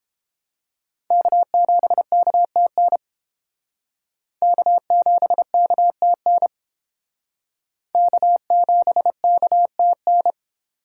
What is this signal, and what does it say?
Morse code, heard as a keyed tone, about 30 wpm, 700 Hz. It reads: K7KTN K7KTN K7KTN